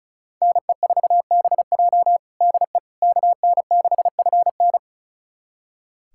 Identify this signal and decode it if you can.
Morse code, heard as a keyed tone, about 36 wpm, 700 Hz. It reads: NE4BJ DE KN6FN
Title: USA Callsign DE Callsign - 5 Characters - 35wpm